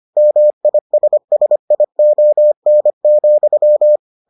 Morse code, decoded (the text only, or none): MISSION,